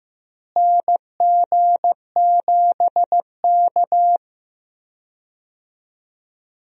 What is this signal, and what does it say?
Morse code, heard as a keyed tone, about 15 wpm, 700 Hz. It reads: NG7K
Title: USA Call Signs - 4 Characters - Repeated 2x - 15wpm